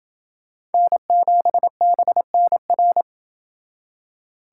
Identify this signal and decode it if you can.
Morse code, heard as a keyed tone, about 30 wpm, 700 Hz. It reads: N7BNR